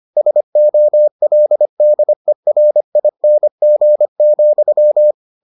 Morse code, heard as a keyed tone, about 25 wpm, 600 Hz. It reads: SOLDERING,